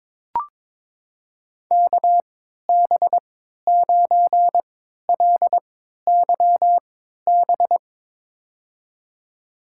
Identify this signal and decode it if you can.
Morse code, heard as a keyed tone, about 22 wpm, spaced as for 13 wpm, 700 Hz. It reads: KB9LYB
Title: USA Call Signs - 6 Characters - 3x Character Spacing - 22wpm